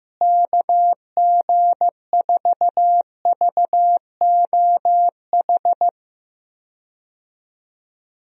Morse code, heard as a keyed tone, about 15 wpm, 700 Hz. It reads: KG4VOH
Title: USA Call Signs - 6 Characters - Repeated 3x - 15wpm